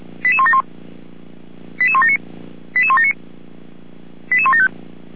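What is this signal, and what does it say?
Five-tone selective call (ZVEI), heard as quick runs of five tones, tones of about 70 ms. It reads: 88171 88179 88179 88175
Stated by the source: ZVEI five-tone selective call 88352, received on SDR.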